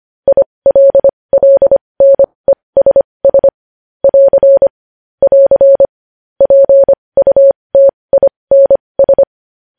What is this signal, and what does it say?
Morse code, heard as a keyed tone, about 25 wpm, spaced as for 20 wpm, 550 Hz. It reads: ILLNESS <AR> <AR> PUTINS
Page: news-headlines-in-morse-code-at-20-wpm